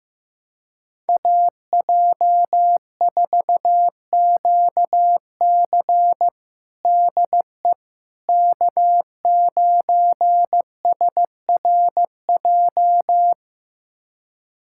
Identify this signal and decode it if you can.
Morse code, heard as a keyed tone, about 16 wpm, 700 Hz. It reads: AJ4QC DE K9SRJ
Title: USA Callsign DE Callsign - 5 Characters - 15wpm